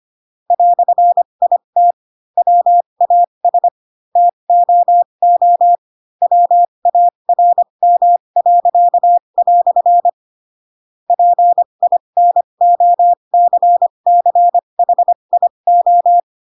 Morse code, heard as a keyed tone, about 25 wpm, spaced as for 22 wpm, 700 Hz. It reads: "IT WAS TOO WARM." PINOCCHIO